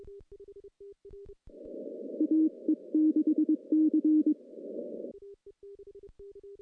Morse code, heard as a keyed tone, about 22 wpm, 310 Hz. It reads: AE6C
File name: CW-SS-k5tr